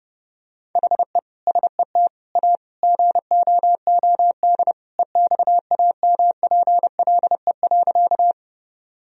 Morse code, heard as a keyed tone, about 30 wpm, 700 Hz. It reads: HE SET A GOOD EXAMPLE.
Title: Sentences from Top 1000 Words - Review Part 1 - 30wpm